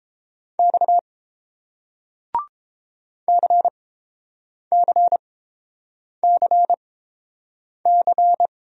Morse code, heard as a keyed tone, about 31 wpm, spaced as for 7 wpm, 700 Hz. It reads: X CCCC